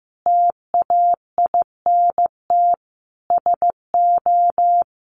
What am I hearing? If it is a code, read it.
Morse code, heard as a keyed tone, about 15 wpm, 700 Hz. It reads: TAINT SO
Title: A Sound Of Thunder 15wpm